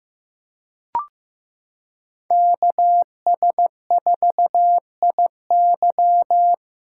Morse code, heard as a keyed tone, about 15 wpm, 700 Hz. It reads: KS4IY
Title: USA Callsign DE Callsign - 5 Characters - 15wpm